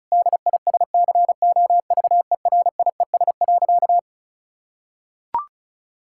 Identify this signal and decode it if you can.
Morse code, heard as a keyed tone, about 36 wpm, 700 Hz. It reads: DISCOVERIES.
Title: Pangram Challenge v2 - 35wpm